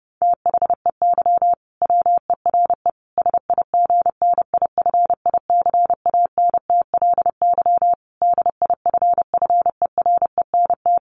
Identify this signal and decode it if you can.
Morse code, heard as a keyed tone, about 30 wpm, 700 Hz. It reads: THEY WERE SIGNIFICANTLY DIFFERENT
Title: Sentences from Top 1500 Words 30wpm